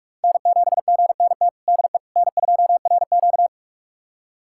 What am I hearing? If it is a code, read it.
Morse code, heard as a keyed tone, about 41 wpm, 700 Hz. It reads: N7GNT DE N2RQ
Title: USA Callsign DE Callsign 45wpm